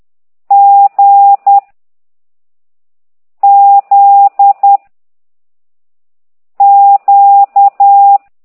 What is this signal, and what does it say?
Morse code, heard as a keyed tone, about 10 wpm, 800 Hz. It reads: G Z Q
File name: ct2idw / exerc11